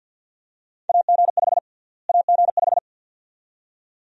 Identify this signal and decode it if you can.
Morse code, heard as a keyed tone, about 50 wpm, 700 Hz. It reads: AG5 AG5
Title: USA Call Sign Prefixes - Repeated 2x - 50wpm